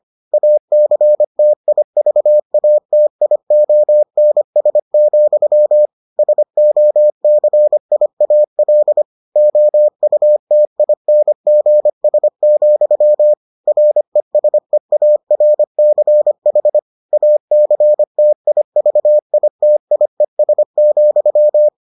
Morse code, heard as a keyed tone, about 25 wpm, 600 Hz. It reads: ACTIVATIONS, SOCIAL OUTINGS, RESEARCH ACTIVITIES,